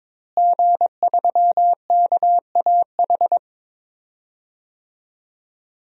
Morse code, heard as a keyed tone, about 22 wpm, 700 Hz. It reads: G3KAH
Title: DXCC-100 Call Signs - 5 Characters - Repeated 2x - 22wpm